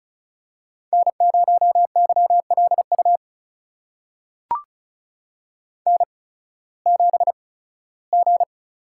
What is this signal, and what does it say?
Morse code, heard as a keyed tone, about 34 wpm, spaced as for 18 wpm, 700 Hz. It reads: N0YLU N7G
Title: USA Call Signs - 5 Characters - 8x Character Spacing - 35wpm